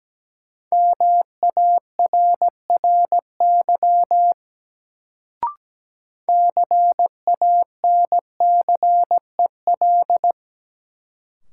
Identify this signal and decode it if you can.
Morse code, heard as a keyed tone, about 17 wpm, 700 Hz. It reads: MARRY CANCEL
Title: Sentences from Top 1200 Words - New Word Drill - 17wpm